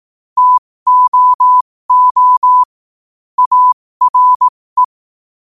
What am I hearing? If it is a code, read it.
Morse code, heard as a keyed tone, about 18 wpm, spaced as for 15 wpm, 1000 Hz. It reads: TOO ARE
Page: 15 WPM morse code quotes